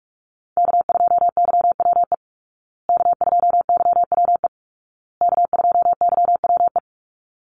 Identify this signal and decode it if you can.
Morse code, heard as a keyed tone, about 44 wpm, 700 Hz. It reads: K2YWE K2YWE K2YWE
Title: Call Signs Repeated 2x - 45wpm